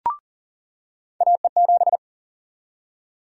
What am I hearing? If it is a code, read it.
Morse code, heard as a keyed tone, about 40 wpm, 700 Hz. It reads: AE7
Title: USA Call Sign Prefixes - 40wpm